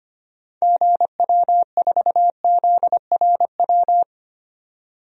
Morse code, heard as a keyed tone, about 25 wpm, 700 Hz. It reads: GW4ZRW
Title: DXCC-100 Call Signs - 6 Characters - 25wpm